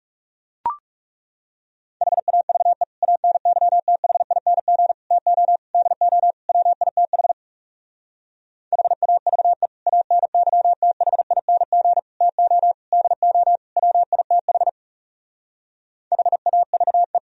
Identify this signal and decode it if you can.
Morse code, heard as a keyed tone, about 37 wpm, 700 Hz. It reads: HAVE ANYTHING TO DO WITH HAVE ANYTHING TO DO WITH HAVE
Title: Speed-Racing - Sets of 5 Words 30wpm